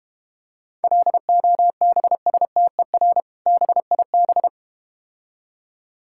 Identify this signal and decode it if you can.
Morse code, heard as a keyed tone, about 32 wpm, 700 Hz. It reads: LOBSTER BIB